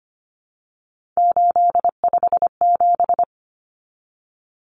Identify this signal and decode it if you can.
Morse code, heard as a keyed tone, about 25 wpm, 700 Hz. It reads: 857
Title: Numbers - 3-Digit 25wpm